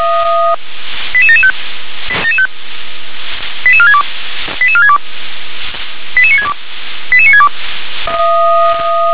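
Five-tone selective call (ZVEI), heard as quick runs of five tones, tones of about 70 ms. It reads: 88775 88462 88462 88962 88962
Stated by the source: Received on an SDR, ZVEI five-tone selective call.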